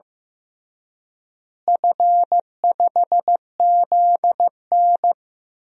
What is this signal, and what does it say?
Morse code, heard as a keyed tone, about 15 wpm, 700 Hz. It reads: F5ZN
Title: DXCC-100 Call Signs - 4 Characters - Repeated 3x - 15wpm